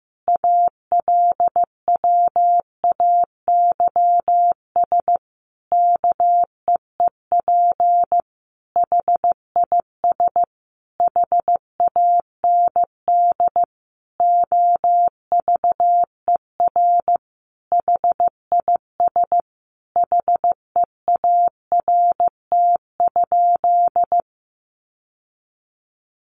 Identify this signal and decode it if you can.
Morse code, heard as a keyed tone, about 15 wpm, 700 Hz. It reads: ALWAYS KEEP HIS HAND OVER HIS HEART?
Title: Sentences from Top 1200 Words 15wpm